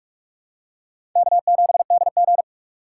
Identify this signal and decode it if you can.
Morse code, heard as a keyed tone, about 46 wpm, 700 Hz. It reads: K7DG
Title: USA Call Signs - 6x Character Spacing - 45wpm